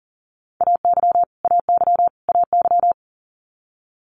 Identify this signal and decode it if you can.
Morse code, heard as a keyed tone, about 39 wpm, 700 Hz. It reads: AY AY AY
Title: Multi-letter Phonemes Repeated 3x 40wpm